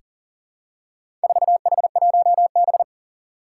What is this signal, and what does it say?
Morse code, heard as a keyed tone, about 40 wpm, 700 Hz. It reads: 4H1B
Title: Four Challenging Letter-Number Combinations 40wpm